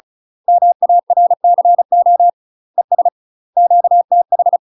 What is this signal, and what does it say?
Morse code, heard as a keyed tone, about 36 wpm, 700 Hz. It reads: MARCO ES QTH
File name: UK MASTER QSOs TRAINING / 35WPM / UK MASTER QSOs 0016